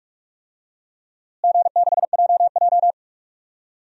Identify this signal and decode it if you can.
Morse code, heard as a keyed tone, about 41 wpm, 700 Hz. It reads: G6JJ
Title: DXCC-100 Call Signs - 4 Characters - Repeated 2x - 45wpm